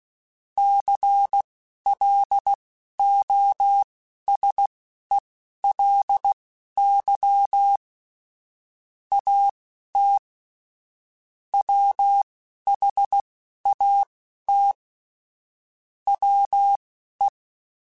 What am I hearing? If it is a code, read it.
Morse code, heard as a keyed tone, about 16 wpm, spaced as for 11 wpm, 800 Hz. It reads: CLOSELY AT WHAT WE
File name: Technical Subjects / Technical Subjects-mp3s / OR Gate @10